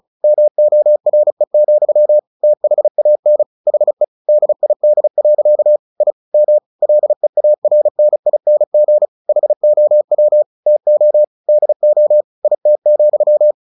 Morse code, heard as a keyed tone, about 34 wpm, 600 Hz. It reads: MORE, THAN HE DID. I M LEARNING HOW TO DO IT,
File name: the old man and the sea 2 35wpm